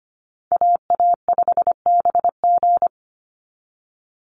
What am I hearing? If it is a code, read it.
Morse code, heard as a keyed tone, about 25 wpm, 700 Hz. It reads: AA5BG
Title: Call Signs Repeated 3x - 25wpm